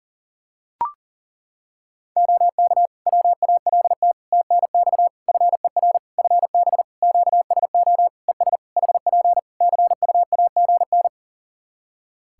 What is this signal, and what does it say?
Morse code, heard as a keyed tone, about 40 wpm, 700 Hz. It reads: OK WALT TNX FER FB QSO ES HP CUAGN
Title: Practice QSO - Exchange 4 - Ending - 40wpm